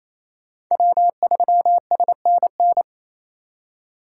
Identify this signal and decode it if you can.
Morse code, heard as a keyed tone, about 35 wpm, 700 Hz. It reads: W3SNN
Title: Speed Racing - USA 1x3 Call Signs 28wpm